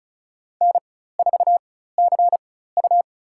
Morse code, heard as a keyed tone, about 36 wpm, spaced as for 16 wpm, 700 Hz. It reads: N4CU
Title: USA Call Signs - 4 Characters - 4x Character Spacing - 35wpm